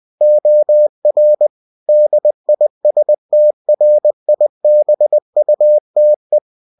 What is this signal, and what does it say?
Morse code, heard as a keyed tone, about 20 wpm, 600 Hz. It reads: OR DISTRIBUTE
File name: cw-books / winnie-the-pooh / winnie-20wpm-page-0116